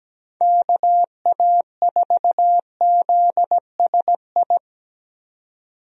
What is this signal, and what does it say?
Morse code, heard as a keyed tone, about 17 wpm, 700 Hz. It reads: KA4ZSI
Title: USA Call Signs - 6 Characters - 17wpm